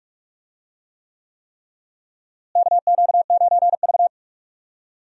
Morse code, heard as a keyed tone, about 42 wpm, 700 Hz. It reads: KQ9V
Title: USA Call Signs - 4 Characters - Repeated 3x - 45wpm